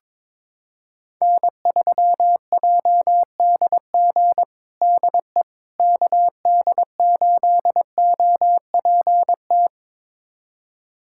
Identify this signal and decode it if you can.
Morse code, heard as a keyed tone, about 22 wpm, 700 Hz. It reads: N3JDG DE KD8OPT
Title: USA Callsign DE Callsign 22wpm